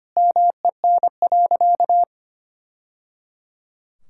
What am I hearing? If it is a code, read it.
Morse code, heard as a keyed tone, about 25 wpm, 700 Hz. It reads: MEN.
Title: Sentences from Top 1000 Words - Review Part 6 - 25wpm